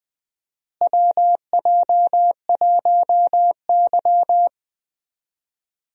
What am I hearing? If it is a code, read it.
Morse code, heard as a keyed tone, about 19 wpm, 700 Hz. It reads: WJ1Y
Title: USA Call Signs - 4 Characters - Repeated 3x - 20wpm